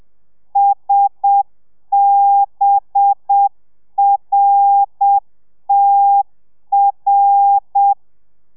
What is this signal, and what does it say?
Morse code, heard as a keyed tone, about 7 wpm, 800 Hz. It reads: SBRTR